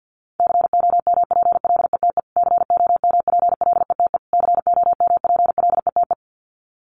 Morse code, heard as a keyed tone, about 50 wpm, 700 Hz. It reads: COMPLETE COMPLETE COMPLETE
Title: Top 500 Words Repeated 50wpm